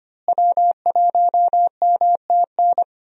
Morse code, heard as a keyed tone, about 25 wpm, 700 Hz. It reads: W1MTN
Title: USA Call Signs - 5 Characters - Repeated 3x - 25wpm